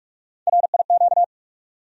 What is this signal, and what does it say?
Morse code, heard as a keyed tone, about 41 wpm, 700 Hz. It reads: RIQ
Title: USA Call Sign Postfixes - 3 Letters 45wpm